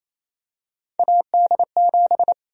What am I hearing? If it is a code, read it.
Morse code, heard as a keyed tone, about 28 wpm, 700 Hz. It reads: AD7
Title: USA Call Sign Prefixes - Repeated 2x - 28wpm